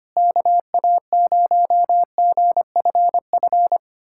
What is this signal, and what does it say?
Morse code, heard as a keyed tone, about 25 wpm, 700 Hz. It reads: KA0GFF
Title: USA Callsign DE Callsign 25wpm